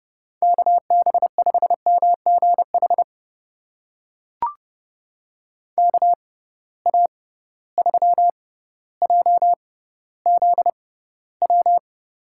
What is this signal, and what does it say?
Morse code, heard as a keyed tone, about 31 wpm, spaced as for 16 wpm, 700 Hz. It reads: KB5MGH KA3JZW
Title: USA Call Signs - 6 Characters - 6x Character Spacing - 30wpm